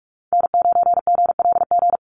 Morse code, heard as a keyed tone, about 44 wpm, 700 Hz. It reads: N9GLG
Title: USA Call Signs Repeated 2x 45wpm